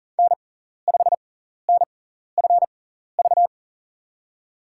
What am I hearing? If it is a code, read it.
Morse code, heard as a keyed tone, about 40 wpm, spaced as for 14 wpm, 700 Hz. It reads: N5NFV
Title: USA Call Signs - 5 Characters - 6x Character Spacing - 40wpm